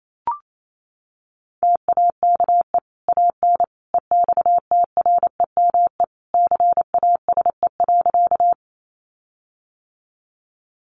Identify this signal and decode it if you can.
Morse code, heard as a keyed tone, about 28 wpm, 700 Hz. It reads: TAKE AN EXTREME CASE.
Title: Sentences from Top 1300 Words 28wpm